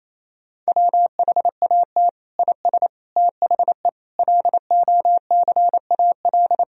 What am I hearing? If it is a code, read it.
Morse code, heard as a keyed tone, about 28 wpm, 700 Hz. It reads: WHAT IS THE LOCAL
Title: Sentences from Top 1000 Words - Review Part 3 - 28wpm